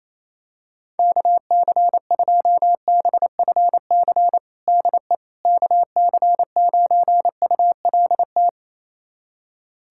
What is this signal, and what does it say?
Morse code, heard as a keyed tone, about 27 wpm, 700 Hz. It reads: KC2BFC DE KC9ULT
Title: USA Callsign DE Callsign - 6 Characters - 28wpm